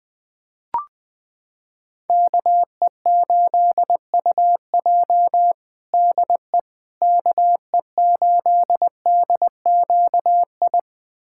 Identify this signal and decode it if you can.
Morse code, heard as a keyed tone, about 21 wpm, 700 Hz. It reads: KE8UJ DE KE8DQI
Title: USA Callsign DE Callsign 20wpm